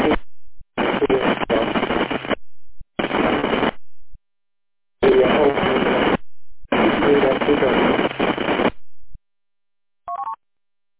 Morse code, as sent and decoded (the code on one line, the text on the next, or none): none